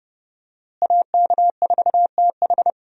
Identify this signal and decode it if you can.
Morse code, heard as a keyed tone, about 28 wpm, 700 Hz. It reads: AK4TH
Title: USA Call Signs - 5 Characters - Repeated 3x - 30wpm